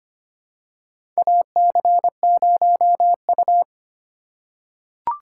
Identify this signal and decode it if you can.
Morse code, heard as a keyed tone, about 25 wpm, 700 Hz. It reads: AC0U
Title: USA Call Signs - 2x Character Spacing - 25wpm